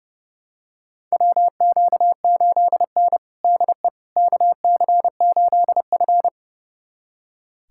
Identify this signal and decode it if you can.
Morse code, heard as a keyed tone, about 30 wpm, 700 Hz. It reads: WQ8N DE KC8F
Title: USA Callsign DE Callsign - 4 Characters - 30wpm